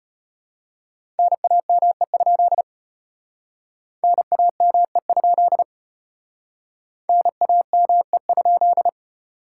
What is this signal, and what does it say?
Morse code, heard as a keyed tone, about 34 wpm, 700 Hz. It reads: NAME? NAME? NAME?